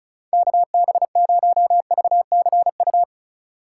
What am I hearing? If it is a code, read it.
Morse code, heard as a keyed tone, about 44 wpm, 700 Hz. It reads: KB0VCU